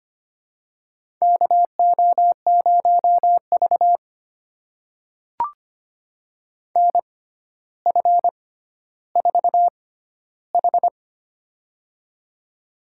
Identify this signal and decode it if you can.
Morse code, heard as a keyed tone, about 25 wpm, spaced as for 13 wpm, 700 Hz. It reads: KO0V NF4H
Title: USA Call Signs - 4 Characters - 6x Character Spacing - 25wpm